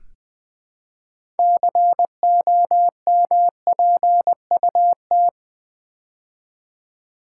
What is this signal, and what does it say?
Morse code, heard as a keyed tone, about 20 wpm, 700 Hz. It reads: COMPUT
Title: Build A Word - 3 Letters At A Time - 8-Letter Words - 20wpm